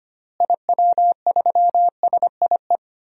Morse code, heard as a keyed tone, about 25 wpm, 700 Hz. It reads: IW3SIE
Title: DXCC-100 Call Signs - 6 Characters - Repeated 3x - 25wpm